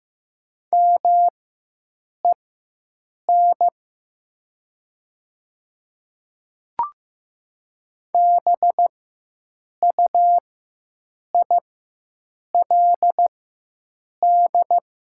Morse code, heard as a keyed tone, about 15 wpm, spaced as for 7 wpm, 700 Hz. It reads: MEN BUILD
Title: 4x Character Spacing - Top 200 Words - 15wpm